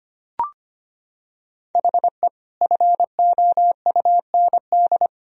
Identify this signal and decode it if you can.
Morse code, heard as a keyed tone, about 25 wpm, 700 Hz. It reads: HE FOUND
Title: Sentences from Top 1000 Words - Review Part 4 - 25wpm